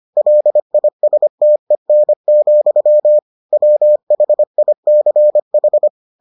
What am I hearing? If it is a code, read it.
Morse code, heard as a keyed tone, about 25 wpm, 600 Hz. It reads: LISTEN, WHICH